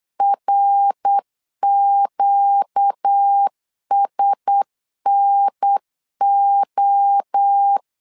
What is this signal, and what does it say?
Morse code, heard as a keyed tone, about 8 wpm, 800 Hz. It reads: RQSNO
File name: LEZIONI / 4 Q1S 8wpm 2710